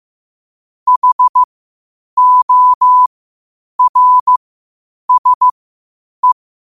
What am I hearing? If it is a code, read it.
Morse code, heard as a keyed tone, about 15 wpm, spaced as for 8 wpm, 1000 Hz. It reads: HORSE
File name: qotdfiles / 8wpm